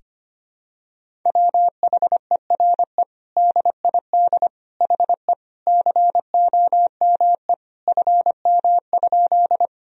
Morse code, heard as a keyed tone, about 25 wpm, 700 Hz. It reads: WHERE DID HE COME FM?
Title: Sentences from Top 200 Words and Abbreviations 25wpm